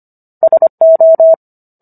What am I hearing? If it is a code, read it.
Morse code, heard as a keyed tone, about 25 wpm, 650 Hz. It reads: SO